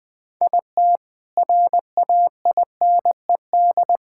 Morse code, heard as a keyed tone, about 20 wpm, 700 Hz. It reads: IT RAINED
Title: Sentences with 7-Word Phrases - Spoken Beginning - 20wpm